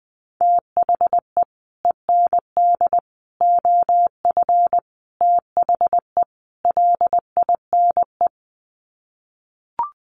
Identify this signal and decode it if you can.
Morse code, heard as a keyed tone, about 20 wpm, 700 Hz. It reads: THE END OF THE LINE
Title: Sets of 5 Words Encore2 20wpm